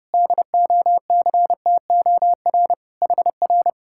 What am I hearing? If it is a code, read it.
Morse code, heard as a keyed tone, about 29 wpm, 700 Hz. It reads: DOCTOR HR